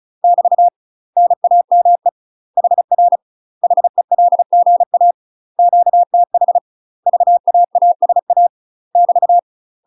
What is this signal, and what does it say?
Morse code, heard as a keyed tone, about 35 wpm, 700 Hz. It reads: <BT> NAME HR HELGA QTH VAASA <BT>
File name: UK MASTER QSOs TRAINING/35WPM